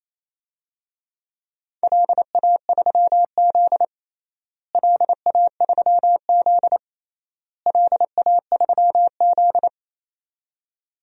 Morse code, heard as a keyed tone, about 28 wpm, 700 Hz. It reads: LA3Z LA3Z LA3Z
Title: DXCC-100 Call Signs - 4 Characters - Repeated 3x - 28wpm